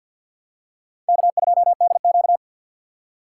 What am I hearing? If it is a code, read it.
Morse code, heard as a keyed tone, about 50 wpm, 700 Hz. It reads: K2DQ
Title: USA Call Signs - 4 Characters - 3x Character Spacing - 50wpm